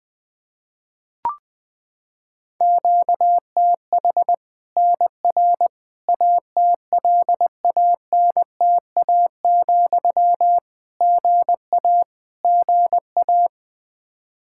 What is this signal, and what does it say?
Morse code, heard as a keyed tone, about 21 wpm, 700 Hz. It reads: QTH NR ATLANTA, GA GA
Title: Practice QSO - Exchange 1 - Beginning - 20wpm